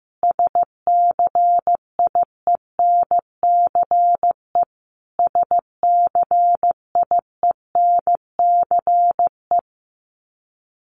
Morse code, heard as a keyed tone, about 15 wpm, 700 Hz. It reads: SCIENCE SCIENCE
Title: Top 500 Words Repeated 15wpm